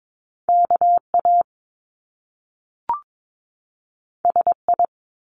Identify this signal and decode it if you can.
Morse code, heard as a keyed tone, about 22 wpm, 700 Hz. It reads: KA SI
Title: Top Two Letter Combinations in Words 22wpm